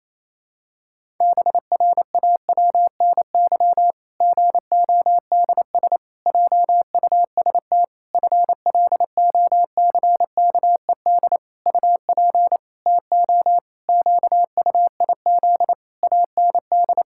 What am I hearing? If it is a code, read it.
Morse code, heard as a keyed tone, about 28 wpm, 700 Hz. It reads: BRAWNY GODS JUST FLOCKED UP TO QUIZ AND
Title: Pangram Challenge v2 - 28wpm